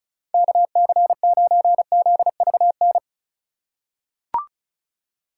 Morse code, heard as a keyed tone, about 34 wpm, 700 Hz. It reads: KC9ZVN
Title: USA Call Signs - 6 Characters - 2x Character Spacing - 35wpm